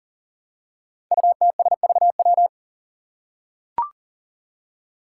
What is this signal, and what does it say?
Morse code, heard as a keyed tone, about 40 wpm, 700 Hz. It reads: UTSVW
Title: Five Challenging Letter-Number Combinations 40wpm